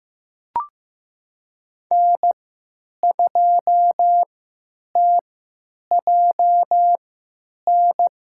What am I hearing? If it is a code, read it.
Morse code, heard as a keyed tone, about 15 wpm, spaced as for 9 wpm, 700 Hz. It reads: N2TJN
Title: USA Call Signs - 3x Character Spacing - 15wpm